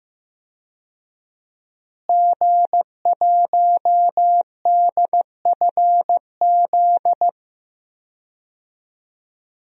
Morse code, heard as a keyed tone, about 15 wpm, 700 Hz. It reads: G1DFZ
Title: DXCC-100 Call Signs - 5 Characters - Repeated 3x - 15wpm